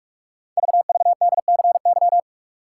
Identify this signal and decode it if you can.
Morse code, heard as a keyed tone, about 44 wpm, 690 Hz. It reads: VVDCY
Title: Code Groups - 5 Letters - 45wpm